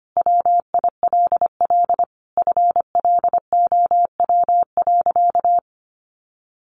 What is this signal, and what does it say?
Morse code, heard as a keyed tone, about 25 wpm, 700 Hz. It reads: WILL FLOW.
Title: Sentences from Top 1200 Words 25wpm